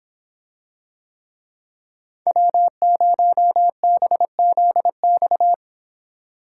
Morse code, heard as a keyed tone, about 26 wpm, 700 Hz. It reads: W0BZX